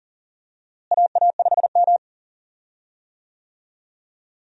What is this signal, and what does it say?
Morse code, heard as a keyed tone, about 40 wpm, 700 Hz. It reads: AA5M
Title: USA Call Signs - 4 Characters - Repeated 2x - 40wpm